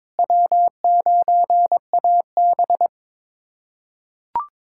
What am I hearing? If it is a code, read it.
Morse code, heard as a keyed tone, about 22 wpm, 700 Hz. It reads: W9AB
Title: USA Callsign DE Callsign - 4 Characters - 22wpm